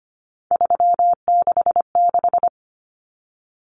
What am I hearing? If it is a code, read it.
Morse code, heard as a keyed tone, about 25 wpm, 690 Hz. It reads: 366
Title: Numbers - 3-Digit 25wpm